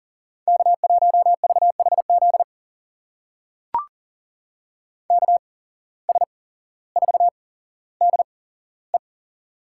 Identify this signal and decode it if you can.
Morse code, heard as a keyed tone, about 40 wpm, spaced as for 17 wpm, 700 Hz. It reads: K1VHZ KS4DE
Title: USA Call Signs - 5 Characters - 8x Character Spacing - 40wpm